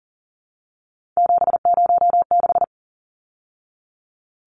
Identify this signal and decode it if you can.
Morse code, heard as a keyed tone, about 40 wpm, 700 Hz. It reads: 706
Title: Numbers - 3-Digit 40wpm